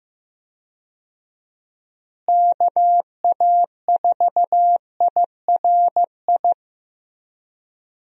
Morse code, heard as a keyed tone, about 15 wpm, 700 Hz. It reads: KA4IRI
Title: USA Call Signs - 6 Characters - Repeated 2x - 15wpm